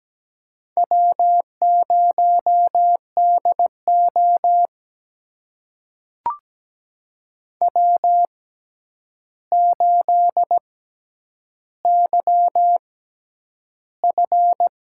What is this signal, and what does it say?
Morse code, heard as a keyed tone, about 17 wpm, spaced as for 9 wpm, 700 Hz. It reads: W0DO W8YF
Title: USA Call Signs - 4 Characters - 6x Character Spacing - 17wpm